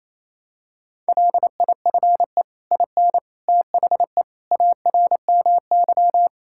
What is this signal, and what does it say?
Morse code, heard as a keyed tone, about 27 wpm, 700 Hz. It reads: LIFE IN THE ARMY
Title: Sentences from Top 1000 Words - Review Part 5 - 28wpm